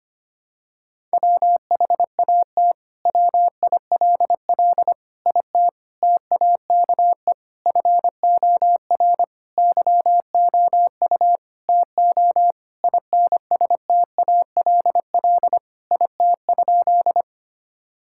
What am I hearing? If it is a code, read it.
Morse code, heard as a keyed tone, about 25 wpm, 700 Hz. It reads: WHAT WILL IT TAKE FOR YOU TO INSTALL IT?
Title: Sentences from Top 1600 Words 25wpm